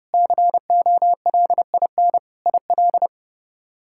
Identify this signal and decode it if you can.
Morse code, heard as a keyed tone, about 32 wpm, 700 Hz. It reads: COLIN IL
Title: K1USN SST QSO Practice - 30wpm